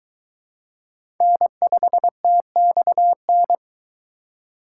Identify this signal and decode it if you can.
Morse code, heard as a keyed tone, about 22 wpm, 700 Hz. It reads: N5TXN